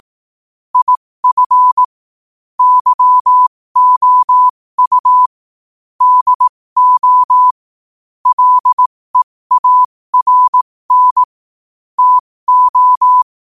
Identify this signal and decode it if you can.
Morse code, heard as a keyed tone, about 18 wpm, spaced as for 15 wpm, 1000 Hz. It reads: IF YOU DO LEARN TO
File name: qotdfiles / 15wpm